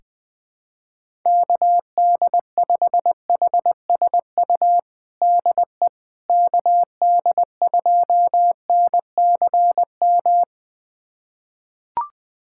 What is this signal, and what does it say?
Morse code, heard as a keyed tone, about 20 wpm, 700 Hz. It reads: KD5HSU DE KD2NCM
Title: USA Callsign DE Callsign - 6 Characters - 20wpm